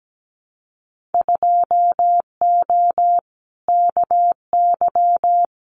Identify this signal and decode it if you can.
Morse code, heard as a keyed tone, about 17 wpm, 700 Hz. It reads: 2O KY
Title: Winter Field Day Exchanges 17wpm